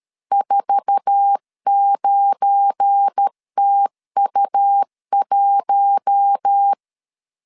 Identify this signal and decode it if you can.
Morse code, heard as a keyed tone, about 13 wpm, 800 Hz. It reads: 49TU1
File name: LEZIONI / 8 YJ45 / 8 YJ45 12wpm 2151